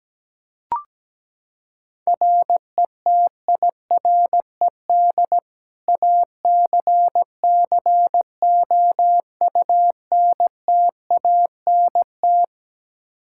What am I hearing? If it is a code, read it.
Morse code, heard as a keyed tone, about 17 wpm, 700 Hz. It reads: RETIRED ACCOUNTANT
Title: Practice QSO - Exchange 3 - More Info - 17wpm